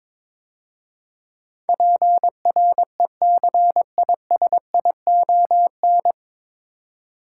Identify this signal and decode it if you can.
Morse code, heard as a keyed tone, about 22 wpm, 700 Hz. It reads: PRECISION
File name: 9 Letter Words - First 3 Letters Given - 22wpm